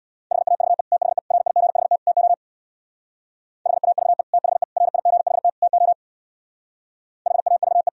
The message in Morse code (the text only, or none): HAVE THE RIGHT TO HAVE THE RIGHT TO HAVE